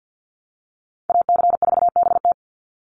Morse code, heard as a keyed tone, about 50 wpm, 700 Hz. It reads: AC4BT
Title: Call Signs - 50wpm